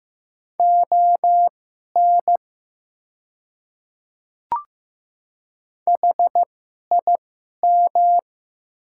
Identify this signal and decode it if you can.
Morse code, heard as a keyed tone, about 15 wpm, spaced as for 11 wpm, 700 Hz. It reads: ON HIM
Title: 2x Character Spacing - Top 100 Words - 15wpm